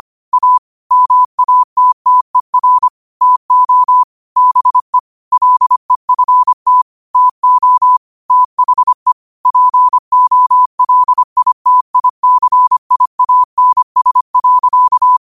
Morse code, heard as a keyed tone, about 25 wpm, 1000 Hz. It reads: A MATTER TO BE LEFT TO THE POLITICIANS.